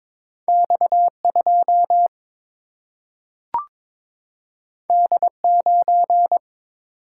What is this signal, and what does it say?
Morse code, heard as a keyed tone, about 22 wpm, 700 Hz. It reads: X 2 D 9
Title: Single-Letter Single-Number 22wpm